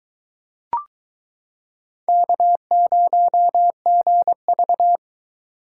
Morse code, heard as a keyed tone, about 22 wpm, 700 Hz. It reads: K0GV